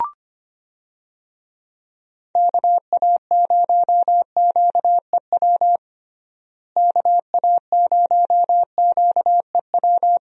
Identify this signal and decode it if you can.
Morse code, heard as a keyed tone, about 25 wpm, 700 Hz. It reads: KA0QEW KA0QEW
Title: USA Call Signs - 6 Characters - Repeated 2x - 25wpm